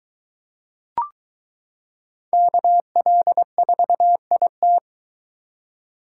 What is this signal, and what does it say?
Morse code, heard as a keyed tone, about 22 wpm, 700 Hz. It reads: KL4IT